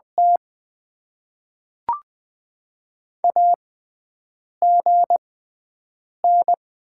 Morse code, heard as a keyed tone, about 20 wpm, spaced as for 7 wpm, 690 Hz. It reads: T AGN
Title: Top 20 QSO Elements - 6x Character Spacing - 20wpm